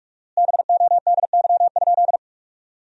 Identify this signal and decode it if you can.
Morse code, heard as a keyed tone, about 46 wpm, 700 Hz. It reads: BODY?
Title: 4x Word-Spacing - Sentences from Top 400 Words - 45wpm